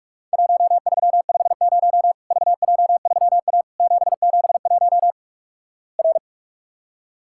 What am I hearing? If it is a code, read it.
Morse code, heard as a keyed tone, about 46 wpm, 700 Hz. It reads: 1350 VJ3A 771 R
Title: Worked All Europe DX Contest - QTC Exchange - 45wpm